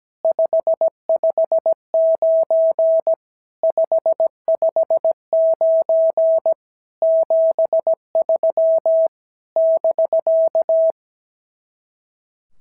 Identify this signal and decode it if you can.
Morse code, heard as a keyed tone, about 17 wpm, 650 Hz. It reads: 559 559 73 <BK>
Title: SOTA QSO Practice 17wpm